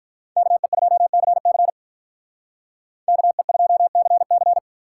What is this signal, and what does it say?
Morse code, heard as a keyed tone, about 46 wpm, 700 Hz. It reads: KE2CC KE2CC